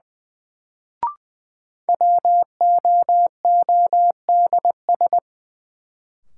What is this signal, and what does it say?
Morse code, heard as a keyed tone, about 20 wpm, 700 Hz. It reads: WOODS
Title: Sentences from Top 1000 Words - Review Part 2 - New Word Drill - 20wpm